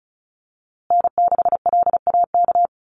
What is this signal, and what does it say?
Morse code, heard as a keyed tone, about 34 wpm, 700 Hz. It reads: N6LAK
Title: USA Call Signs Repeated 2x 35wpm